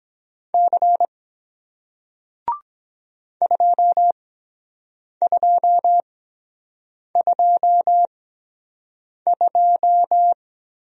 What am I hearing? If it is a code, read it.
Morse code, heard as a keyed tone, about 23 wpm, spaced as for 7 wpm, 700 Hz. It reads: C 2222